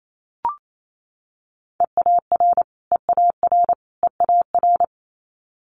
Morse code, heard as a keyed tone, about 28 wpm, 700 Hz. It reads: EAR EAR EAR
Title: Multi-letter Phonemes Repeated 3x 28wpm